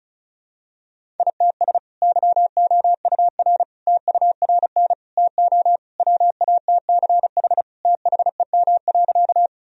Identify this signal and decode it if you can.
Morse code, heard as a keyed tone, about 36 wpm, 700 Hz. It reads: ITS YOUR TURN TO WATCH THEM.
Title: Sentences from Top 1000 Words - Review Part 1 - 35wpm